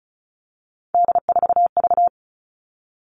Morse code, heard as a keyed tone, about 34 wpm, 700 Hz. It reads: D4V
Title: All Cut Numbers - 3-Digit 35wpm